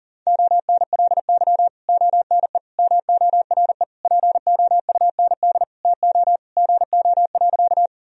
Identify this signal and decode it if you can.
Morse code, heard as a keyed tone, about 40 wpm, 700 Hz. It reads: ONLY ONE MORE POUND TO GO.
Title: Sentences from Top 1000 Words - Review Part 2 - 40wpm